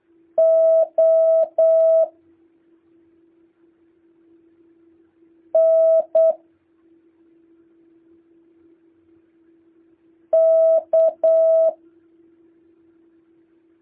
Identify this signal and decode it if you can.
Morse code, heard as a keyed tone, about 8 wpm, spaced as for 2 wpm, 650 Hz. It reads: ONK